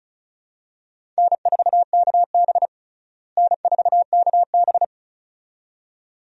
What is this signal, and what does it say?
Morse code, heard as a keyed tone, about 36 wpm, 700 Hz. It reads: N4KB N4KB
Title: USA Call Signs - 4 Characters - Repeated 2x - 35wpm